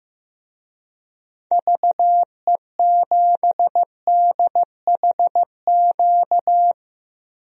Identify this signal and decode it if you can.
Morse code, heard as a keyed tone, about 15 wpm, 700 Hz. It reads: VE7DHQ
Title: DXCC-100 Call Signs - 6 Characters - Repeated 3x - 15wpm